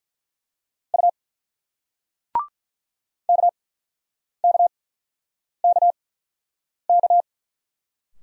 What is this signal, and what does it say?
Morse code, heard as a keyed tone, about 47 wpm, spaced as for 9 wpm, 700 Hz. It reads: U KKKK